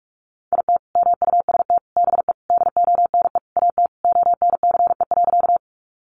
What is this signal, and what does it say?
Morse code, heard as a keyed tone, about 44 wpm, 700 Hz. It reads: IT MUST BE DONE AT ONCE.
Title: Sentences from Top 500 Words 45wpm